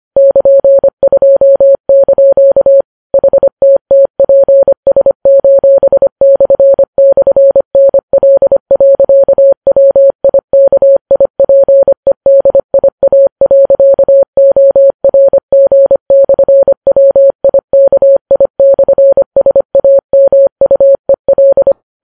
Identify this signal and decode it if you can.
Morse code, heard as a keyed tone, about 25 wpm, 550 Hz. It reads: <KN>2) HTTPS://NL.WIKIPEDIA.ORG/WIKI/SAMUEL